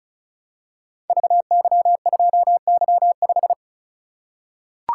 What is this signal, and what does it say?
Morse code, heard as a keyed tone, about 36 wpm, 700 Hz. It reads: VY2Y5
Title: Code Groups - 5 Letter:Numbers - 35wpm